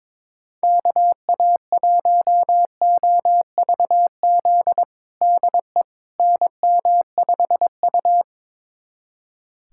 Morse code, heard as a keyed tone, about 22 wpm, 700 Hz. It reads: KA1OVZ DE NM5U
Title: USA Callsign DE Callsign 22wpm